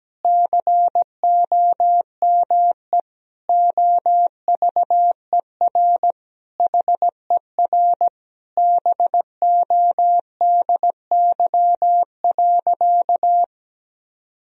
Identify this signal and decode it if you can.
Morse code, heard as a keyed tone, about 17 wpm, 700 Hz. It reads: COME OVER HER BODY.
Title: Sentences from Top 1000 Words - Review Part 2 - 17wpm